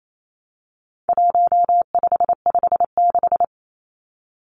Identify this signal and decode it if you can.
Morse code, heard as a keyed tone, about 28 wpm, 700 Hz. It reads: 1556
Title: Numbers - 4-Digit 28wpm